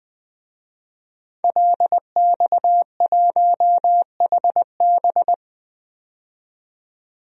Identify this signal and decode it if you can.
Morse code, heard as a keyed tone, about 20 wpm, 700 Hz. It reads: LX1HB
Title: DXCC-100 Call Signs - 5 Characters - 20wpm